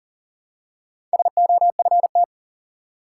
Morse code, heard as a keyed tone, about 40 wpm, 700 Hz. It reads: SOFT
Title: Four Letter Words - 40wpm